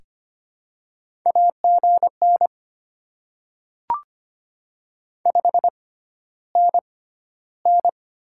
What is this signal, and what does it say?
Morse code, heard as a keyed tone, about 25 wpm, spaced as for 13 wpm, 700 Hz. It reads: AGN 5NN
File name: Top 20 QSO Elements - 6x Character Spacing - 25wpm